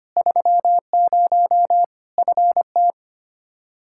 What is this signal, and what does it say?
Morse code, heard as a keyed tone, about 25 wpm, 700 Hz. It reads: 30 FT
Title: Practice QSO - Exchange 2 - Info - 25wpm